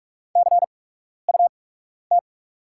Morse code, heard as a keyed tone, about 44 wpm, spaced as for 12 wpm, 690 Hz. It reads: CUT
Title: 8x Character Spacing - Top 500 Words - 45wpm